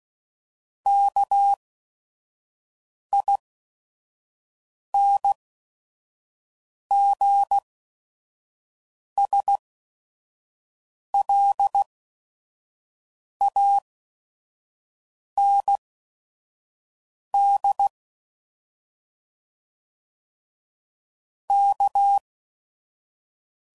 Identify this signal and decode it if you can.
Morse code, heard as a keyed tone, about 16 wpm, spaced as for 5 wpm, 800 Hz. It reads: KINGSLAND K
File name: CW Code Groups @5WPM / fromSM's / Cities